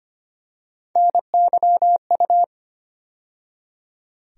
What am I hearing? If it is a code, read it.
Morse code, heard as a keyed tone, about 25 wpm, 700 Hz. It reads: NYU
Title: USA Call Sign Postfixes - 3 Letters 25wpm